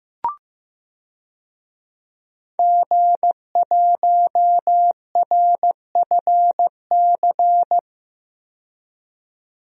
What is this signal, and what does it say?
Morse code, heard as a keyed tone, about 16 wpm, 700 Hz. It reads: G1RFC
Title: DXCC-100 Call Signs - 5 Characters - 15wpm